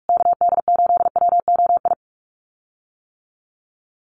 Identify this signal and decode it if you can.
Morse code, heard as a keyed tone, about 44 wpm, 690 Hz. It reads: KD8WOI
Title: USA Call Signs Repeated 2x 45wpm